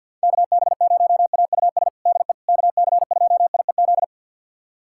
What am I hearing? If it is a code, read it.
Morse code, heard as a keyed tone, about 46 wpm, 700 Hz. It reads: KB0AUS DE KC2IEZ
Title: USA Callsign DE Callsign 50wpm